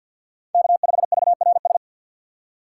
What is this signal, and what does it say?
Morse code, heard as a keyed tone, about 48 wpm, 700 Hz. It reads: K5VRS
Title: USA Call Signs - 6x Character Spacing - 50wpm